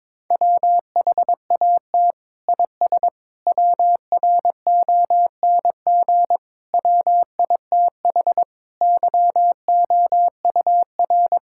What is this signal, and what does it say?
Morse code, heard as a keyed tone, about 22 wpm, 700 Hz. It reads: WHAT IS WRONG WITH YOUR
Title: Sentences from Top 1000 Words - Review Part 5 - 22wpm